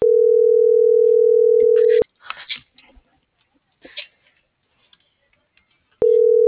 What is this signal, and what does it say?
Ringback tone (North America), 440 + 480 Hz.